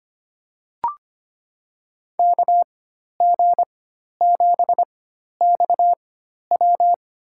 Morse code, heard as a keyed tone, about 25 wpm, spaced as for 12 wpm, 700 Hz. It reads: KG7XW
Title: USA Call Signs - 5 Characters - 4x Character Spacing - 25wpm